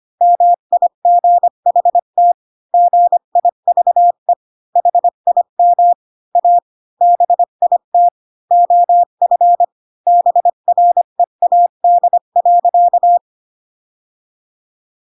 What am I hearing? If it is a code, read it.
Morse code, heard as a keyed tone, about 25 wpm, 700 Hz. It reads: MIGHT GIVE HIM A BIT OF BREAD.
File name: Pinocchio / Pinocchio ch5 20 25 ik1hge